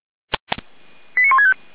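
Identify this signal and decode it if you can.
Five-tone selective call (ZVEI), heard as quick runs of five tones, tones of about 70 ms. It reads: 89165
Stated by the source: ZVEI five-tone selective call 89167 received by SDR, Feuerwehr Orlamünde.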